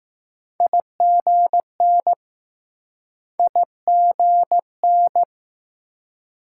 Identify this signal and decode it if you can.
Morse code, heard as a keyed tone, about 20 wpm, 700 Hz. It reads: IGN IGN